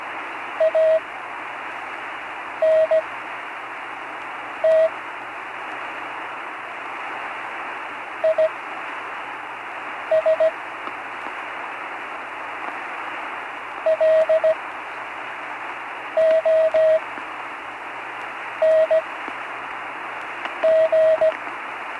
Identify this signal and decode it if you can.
Morse code, heard as a keyed tone, about 17 wpm, spaced as for 5 wpm, 640 Hz, with radio noise. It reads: ANT IS LONG
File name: CW Learning Programs / Learning Morse Code (from Tune in the World) / 43 on the air